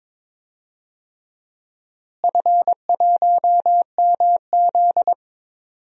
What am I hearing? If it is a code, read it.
Morse code, heard as a keyed tone, about 21 wpm, 700 Hz. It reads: F1MZ
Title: DXCC-100 Call Signs - 4 Characters - Repeated 3x - 22wpm